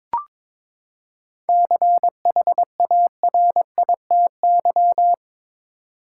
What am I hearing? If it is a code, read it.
Morse code, heard as a keyed tone, about 22 wpm, 700 Hz. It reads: CHARITY
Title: Seven Letter Words - 22wpm